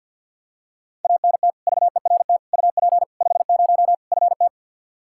Morse code, heard as a keyed tone, about 52 wpm, 700 Hz. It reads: ANT VERT UP 50 FT
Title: Practice QSO - Exchange 2 - Info - 50wpm